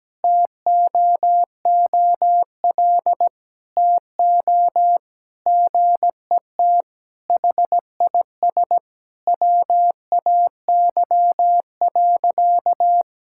Morse code, heard as a keyed tone, about 17 wpm, 700 Hz. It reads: TOOL TO GET HIS WAY.
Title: Sentences from Top 1000 Words - Review Part 6 - 17wpm